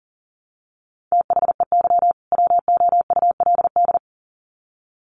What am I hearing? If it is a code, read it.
Morse code, heard as a keyed tone, about 40 wpm, 700 Hz. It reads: THEY WOULD
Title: Sets of 2 Words 40wpm